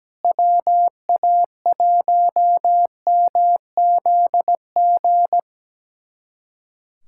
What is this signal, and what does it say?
Morse code, heard as a keyed tone, about 17 wpm, 700 Hz. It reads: WA1MZG
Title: USA Callsign DE Callsign 17wpm